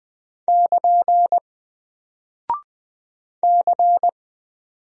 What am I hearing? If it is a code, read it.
Morse code, heard as a keyed tone, about 20 wpm, 700 Hz. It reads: <KN> C
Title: Single Character (Letter:Number:Punctuation:Prosign) 20wpm